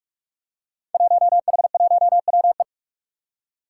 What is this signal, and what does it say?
Morse code, heard as a keyed tone, about 44 wpm, 700 Hz. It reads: 1H1WE
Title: Five Challenging Letter-Number Combinations 45wpm